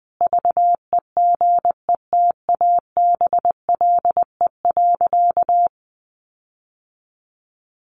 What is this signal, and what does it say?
Morse code, heard as a keyed tone, about 20 wpm, 700 Hz. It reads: VEGETABLE.
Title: Sentences from Top 1300 Words 20wpm